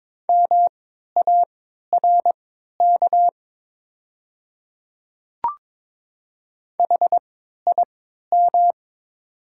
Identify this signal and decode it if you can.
Morse code, heard as a keyed tone, about 22 wpm, spaced as for 12 wpm, 700 Hz. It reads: MARK HIM
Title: 3x Character Spacing - Top 300 Words - 22wpm